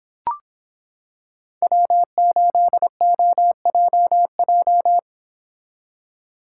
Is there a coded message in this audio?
Yes: Morse code from 1.6 to 5.0 s, heard as a keyed tone, about 25 wpm, 700 Hz. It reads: W8OJJ